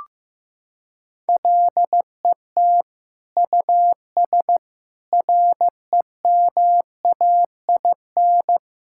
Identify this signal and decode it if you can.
Morse code, heard as a keyed tone, about 15 wpm, 700 Hz. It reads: LET US REMAIN
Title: Sentences from Top 1600 Words 15wpm